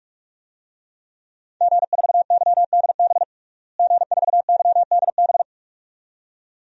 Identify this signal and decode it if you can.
Morse code, heard as a keyed tone, about 46 wpm, 700 Hz. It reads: G4YDB G4YDB
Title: DXCC-100 Call Signs - 5 Characters - Repeated 2x - 45wpm